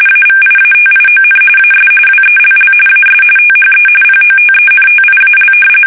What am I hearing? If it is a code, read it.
RTTY, about 45.45 baud. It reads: AVIATION AS WELL AS THE POTENTIAL
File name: fsk850